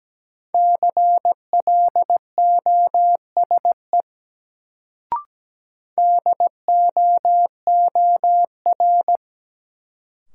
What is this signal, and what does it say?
Morse code, heard as a keyed tone, about 17 wpm, 700 Hz. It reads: CLOSE DOOR
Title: Sentences from Top 300 Words - New Word Drill - 17wpm